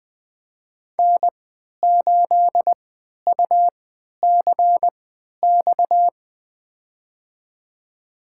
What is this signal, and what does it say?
Morse code, heard as a keyed tone, about 20 wpm, spaced as for 11 wpm, 700 Hz. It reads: N8UCX
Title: USA Call Signs - 5 Characters - 3x Character Spacing - 20wpm